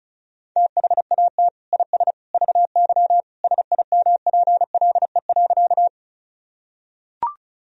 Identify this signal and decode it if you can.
Morse code, heard as a keyed tone, about 36 wpm, 700 Hz. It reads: THAT IS VY SIMPLE.
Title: Sentences from Top 400 Words and Abbreviations 35wpm